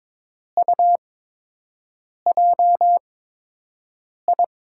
Morse code, heard as a keyed tone, about 22 wpm, spaced as for 6 wpm, 700 Hz. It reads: UJI
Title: USA Call Signs - 5 Characters - 8x Character Spacing - 22wpm